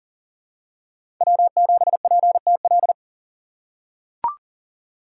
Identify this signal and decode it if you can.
Morse code, heard as a keyed tone, about 40 wpm, 700 Hz. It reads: W7PTL
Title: USA Call Signs - 5 Characters - 4x Character Spacing - 40wpm